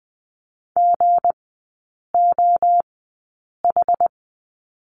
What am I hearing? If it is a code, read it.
Morse code, heard as a keyed tone, about 19 wpm, spaced as for 8 wpm, 690 Hz. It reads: GOH
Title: 212 Lesson 34e - Number 8 - Three Characters 20wpm